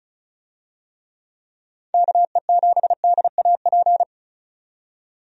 Morse code, heard as a keyed tone, about 35 wpm, 700 Hz. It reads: KE7DAP
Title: USA Call Signs - 6 Characters - 35wpm